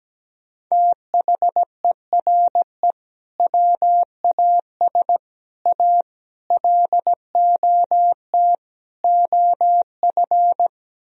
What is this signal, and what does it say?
Morse code, heard as a keyed tone, about 17 wpm, 700 Hz. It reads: THERE WAS A LOT OF
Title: Speed-Racing - Sets of 5 Words 17wpm